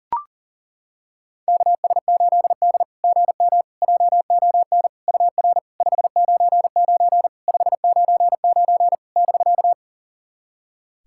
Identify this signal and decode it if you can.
Morse code, heard as a keyed tone, about 40 wpm, 700 Hz. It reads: KS8D GM JON UR 599 599 <BK>
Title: SOTA QSO Practice 40wpm